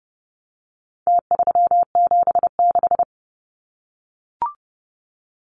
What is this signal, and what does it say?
Morse code, heard as a keyed tone, about 30 wpm, 700 Hz. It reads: T376
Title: Numbers - Contest Style - 4-Digit 30wpm